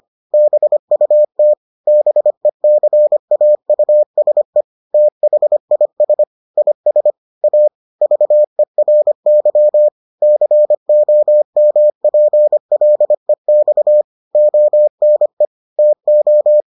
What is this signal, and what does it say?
Morse code, heard as a keyed tone, about 25 wpm, 600 Hz. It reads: BUT BECAUSE THIS IS A VERY COMPLEX ONE TO